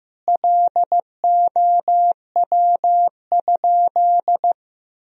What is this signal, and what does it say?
Morse code, heard as a keyed tone, about 16 wpm, 700 Hz. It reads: LOW?
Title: Sentences from Top 1000 Words - Review Part 1 - 15wpm